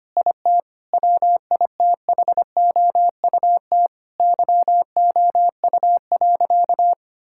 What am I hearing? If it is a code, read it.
Morse code, heard as a keyed tone, about 25 wpm, 700 Hz. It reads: IT WITHOUT YOU.
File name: practice / 25wpm / Difficult Sentences with Hints - 25wpm